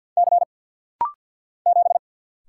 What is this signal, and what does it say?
Morse code, heard as a keyed tone, about 50 wpm, 700 Hz. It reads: C 7
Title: Single Letter-Number Alt - Rapid-Fire 50wpm